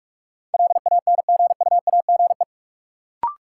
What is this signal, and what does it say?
Morse code, heard as a keyed tone, about 42 wpm, 700 Hz. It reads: LANGUAGE
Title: Sentences from Top 500 Words - New Word Drill - 45wpm